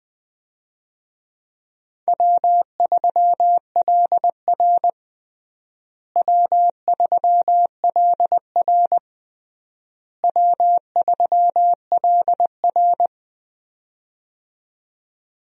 Morse code, heard as a keyed tone, about 20 wpm, 700 Hz. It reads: W3LR W3LR W3LR
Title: USA Call Signs - 4 Characters - Repeated 3x - 20wpm